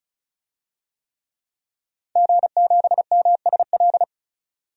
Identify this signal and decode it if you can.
Morse code, heard as a keyed tone, about 34 wpm, 700 Hz. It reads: G7MSL
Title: DXCC-100 Call Signs - 5 Characters - 35wpm